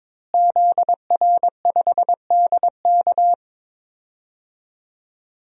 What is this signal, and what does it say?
Morse code, heard as a keyed tone, about 22 wpm, 700 Hz. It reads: ZR5DK
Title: DXCC-100 Call Signs - 5 Characters - Repeated 2x - 22wpm